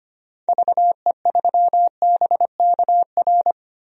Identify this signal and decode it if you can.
Morse code, heard as a keyed tone, about 25 wpm, 700 Hz. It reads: VE3BKR
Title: DXCC-100 Call Signs - 6 Characters - 25wpm